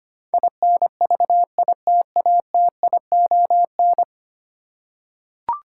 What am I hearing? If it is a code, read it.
Morse code, heard as a keyed tone, about 25 wpm, 700 Hz. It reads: INVITATION
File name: Ten Letter Words - 25wpm